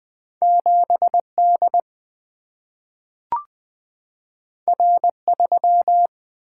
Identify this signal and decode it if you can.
Morse code, heard as a keyed tone, about 20 wpm, 700 Hz. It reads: 7 D R 3
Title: Single-Letter Single-Number 20wpm